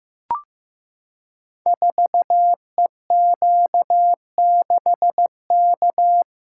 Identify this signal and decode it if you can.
Morse code, heard as a keyed tone, about 15 wpm, 700 Hz. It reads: 4EQ6K
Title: Code Groups - 5 Letter:Numbers - 15wpm